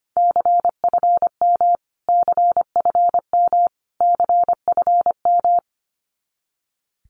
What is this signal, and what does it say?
Morse code, heard as a keyed tone, about 25 wpm, 700 Hz. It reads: CFM CFM CFM
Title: Abbreviations Repeated 3x - 25wpm